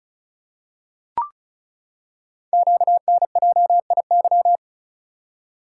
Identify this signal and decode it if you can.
Morse code, heard as a keyed tone, about 36 wpm, 690 Hz. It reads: QNJIY
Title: Code Groups - 5 Letters - 35wpm